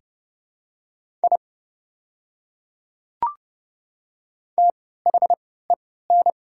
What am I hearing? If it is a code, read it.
Morse code, heard as a keyed tone, about 30 wpm, spaced as for 17 wpm, 700 Hz. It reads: I THEN
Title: 3x Character Spacing - Top 100 Words - 30wpm